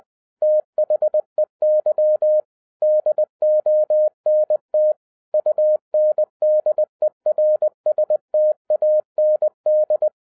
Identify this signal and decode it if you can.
Morse code, heard as a keyed tone, about 20 wpm, 610 Hz. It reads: THEY DONT UNDERSTAND